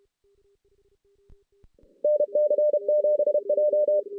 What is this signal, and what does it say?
Morse code, heard as a keyed tone, about 29 wpm, 600 Hz. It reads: NC7J
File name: 2016 JAN CW NAQP